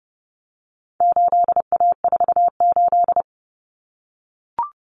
Morse code, heard as a keyed tone, about 30 wpm, 700 Hz. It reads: 8A48
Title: Common Cut Numbers - 4-Digit 30wpm